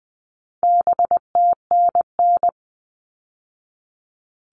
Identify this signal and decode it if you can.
Morse code, heard as a keyed tone, about 20 wpm, 700 Hz. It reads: BTNN
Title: All Cut Numbers - 4-Digit 20wpm